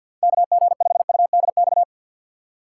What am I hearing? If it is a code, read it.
Morse code, heard as a keyed tone, about 52 wpm, 700 Hz. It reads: KG5UDX
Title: USA Call Signs - 4x Character Spacing - 50wpm